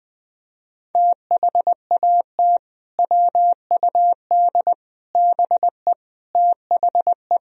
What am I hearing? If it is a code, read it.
Morse code, heard as a keyed tone, about 20 wpm, 700 Hz. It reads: THAT WUD BE THE
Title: Sentences from Top 400 Words and Abbreviations 20wpm